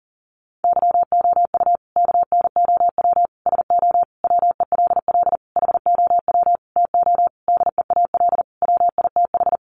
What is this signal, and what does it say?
Morse code, heard as a keyed tone, about 40 wpm, 700 Hz. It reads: YOU KNOW SO WELL HOW TO DEAL WITH
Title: Sentences from Top 800 Words 40wpm